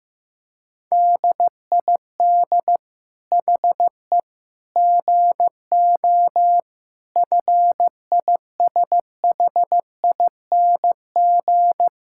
Morse code, heard as a keyed tone, about 15 wpm, 700 Hz. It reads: DID HE GO FISHING
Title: Sentences from Top 1000 Words - Review Part 5 - 15wpm